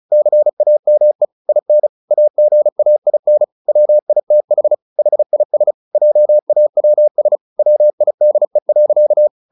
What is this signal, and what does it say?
Morse code, heard as a keyed tone, about 34 wpm, 600 Hz. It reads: CAME IN AGAIN WITH HIS JAWS WIDE.
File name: the old man and the sea 2 35wpm